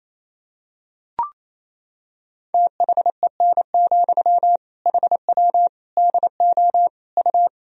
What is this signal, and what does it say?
Morse code, heard as a keyed tone, about 28 wpm, 700 Hz. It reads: THEN, HW DO U
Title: Sentences from Top 100 Words and Abbreviations 28wpm